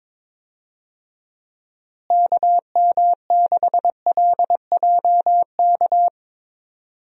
Morse code, heard as a keyed tone, about 22 wpm, 700 Hz. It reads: KM6LJK
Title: USA Call Signs - 6 Characters - Repeated 2x - 22wpm